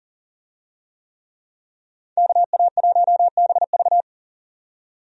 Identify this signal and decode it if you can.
Morse code, heard as a keyed tone, about 40 wpm, 700 Hz. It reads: KA1BV
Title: USA Call Signs - 5 Characters - 40wpm